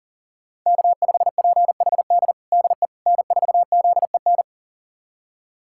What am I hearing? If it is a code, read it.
Morse code, heard as a keyed tone, about 40 wpm, 700 Hz. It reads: K5PHD DE N4ZEN
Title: USA Callsign DE Callsign - 5 Characters - 40wpm